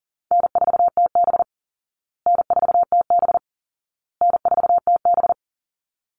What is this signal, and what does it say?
Morse code, heard as a keyed tone, about 40 wpm, 700 Hz. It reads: N4TB N4TB N4TB
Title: Call Signs Repeated 2x - 40wpm